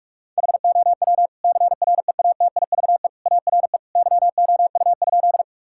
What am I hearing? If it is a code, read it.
Morse code, heard as a keyed tone, about 46 wpm, 700 Hz. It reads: HOW CREATIVE ARE YOU?
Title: Sentences from Top 1000 Words - Review Part 5 - 45wpm